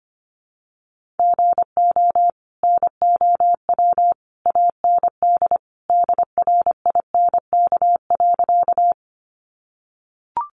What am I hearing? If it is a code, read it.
Morse code, heard as a keyed tone, about 25 wpm, 700 Hz. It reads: GO NOW AND DRINK.
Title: Sentences from Top 1500 Words 25wpm